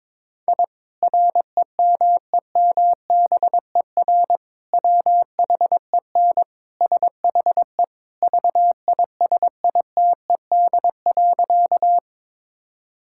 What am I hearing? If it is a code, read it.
Morse code, heard as a keyed tone, about 22 wpm, 700 Hz. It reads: I REMEMBER WHEN SHE VISITED.
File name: practice / 22wpm / Sentences with 2-Word Phrases - 22wpm